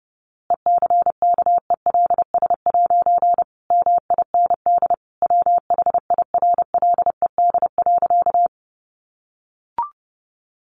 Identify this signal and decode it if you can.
Morse code, heard as a keyed tone, about 30 wpm, 700 Hz. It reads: ECKELS' MIND WHIRLED.
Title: A Sound Of Thunder 30wpm